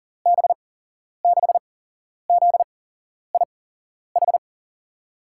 Morse code, heard as a keyed tone, about 38 wpm, spaced as for 11 wpm, 700 Hz. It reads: B6ZIH
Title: USA Call Signs - 6 Characters - 8x Character Spacing - 40wpm